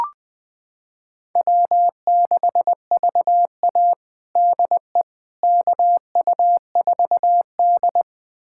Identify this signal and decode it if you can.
Morse code, heard as a keyed tone, about 20 wpm, 700 Hz. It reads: W6VA DE KU4D
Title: USA Callsign DE Callsign - 4 Characters - 20wpm